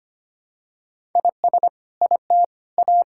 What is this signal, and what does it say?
Morse code, heard as a keyed tone, about 25 wpm, 700 Hz. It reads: IS IT A
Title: Sentences from Top 1000 Words - Review Part 6 - 25wpm